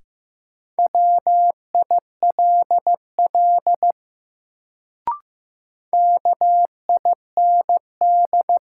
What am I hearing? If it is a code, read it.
Morse code, heard as a keyed tone, about 15 wpm, 700 Hz. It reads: WILL KIND
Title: Sets of 2 Words - New Word Drill - 15wpm